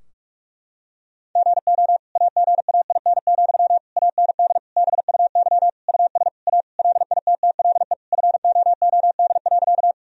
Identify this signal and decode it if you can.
Morse code, heard as a keyed tone, about 46 wpm, 700 Hz. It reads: GO AGAIN, AND BUY US A LITTLE FOOD.
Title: Sentences from Top 1000 Words - Review Part 4 - 45wpm